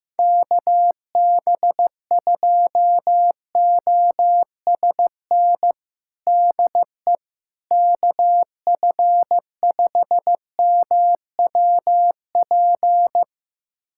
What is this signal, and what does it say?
Morse code, heard as a keyed tone, about 15 wpm, 700 Hz. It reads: KB2OSN DE KF5MWP
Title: USA Callsign DE Callsign 15wpm